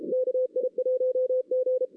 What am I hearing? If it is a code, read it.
Morse code, heard as a keyed tone, about 33 wpm, 510 Hz. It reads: KI1G
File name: 2016 arrl dx cw / 2016-ARRL-DX-CW-K5TR- - 0-3